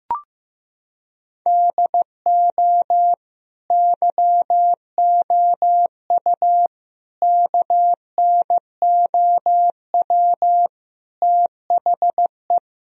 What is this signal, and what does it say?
Morse code, heard as a keyed tone, about 15 wpm, 700 Hz. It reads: DO YOU KNOW THE
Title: Sentences from Top 1000 Words - Review Part 4 - 15wpm